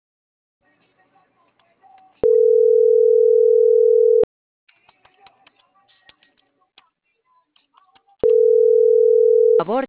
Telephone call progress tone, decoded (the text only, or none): ringback tone